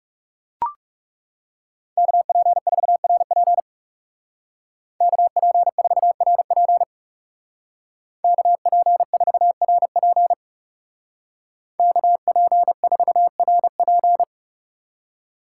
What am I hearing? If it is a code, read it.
Morse code, heard as a keyed tone, about 39 wpm, 700 Hz. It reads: KP4RP KP4RP KP4RP KP4RP